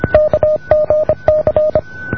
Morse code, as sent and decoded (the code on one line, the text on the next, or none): -.- --. -.-.
KGC